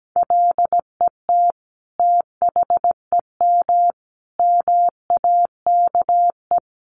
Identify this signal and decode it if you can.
Morse code, heard as a keyed tone, about 17 wpm, 700 Hz. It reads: LET THEM MAKE
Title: Sentences from Top 1300 Words 17wpm